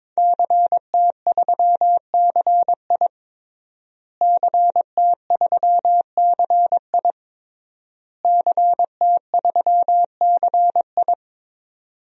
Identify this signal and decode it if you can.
Morse code, heard as a keyed tone, about 21 wpm, 700 Hz. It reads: CT3CI CT3CI CT3CI
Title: DXCC-100 Call Signs - 5 Characters - Repeated 3x - 22wpm